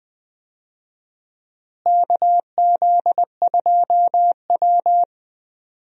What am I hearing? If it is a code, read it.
Morse code, heard as a keyed tone, about 21 wpm, 700 Hz. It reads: KZ2W
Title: USA Call Signs - 4 Characters - Repeated 3x - 20wpm